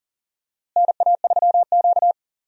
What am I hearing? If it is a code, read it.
Morse code, heard as a keyed tone, about 40 wpm, 700 Hz. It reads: NA3Q
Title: USA Call Signs - 4 Characters - Repeated 2x - 40wpm